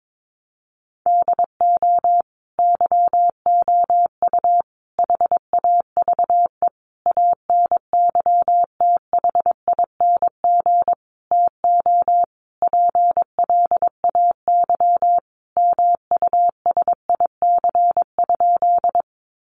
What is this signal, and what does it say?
Morse code, heard as a keyed tone, about 22 wpm, 700 Hz. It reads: DO YOU HAVE ANYTHING TO PLAY MUSIC?
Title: Sentences from Top 1000 Words 22wpm